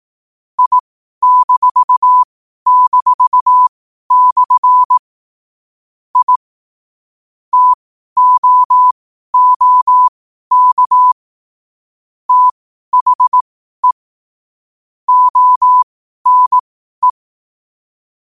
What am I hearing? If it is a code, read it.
Morse code, heard as a keyed tone, about 18 wpm, spaced as for 12 wpm, 1000 Hz. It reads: I--/ I TOOK THE ONE
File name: qotdfiles / 12wpm